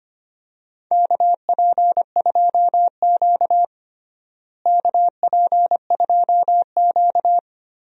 Morse code, heard as a keyed tone, about 25 wpm, 700 Hz. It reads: KP2Q KP2Q
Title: DXCC-100 Call Signs - 4 Characters - Repeated 2x - 25wpm